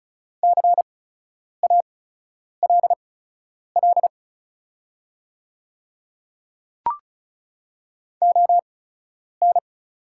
Morse code, heard as a keyed tone, about 34 wpm, spaced as for 10 wpm, 700 Hz. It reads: CALL ON
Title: 8x Character Spacing - Top 100 Words - 35wpm